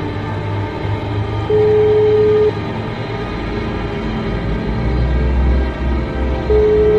Ringback tone (Europe, 425 Hz), with noise.